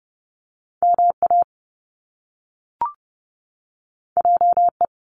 Morse code, heard as a keyed tone, about 30 wpm, 700 Hz. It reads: MA JE
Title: Top Two Letter Combinations in Words 30wpm